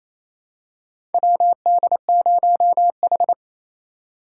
Morse code, heard as a keyed tone, about 28 wpm, 700 Hz. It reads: WD0H
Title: USA Call Signs - 4 Characters - 4x Character Spacing - 28wpm